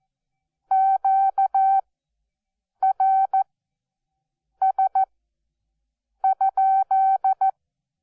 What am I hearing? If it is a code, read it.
Morse code, heard as a keyed tone, about 14 wpm, spaced as for 6 wpm, 770 Hz. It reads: QRS?
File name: Onlinelehrgang / morsen / DJ4UF Morsen12